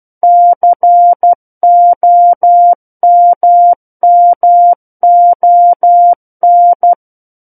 Morse code, heard as a keyed tone, about 12 wpm, 700 Hz. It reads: COMMON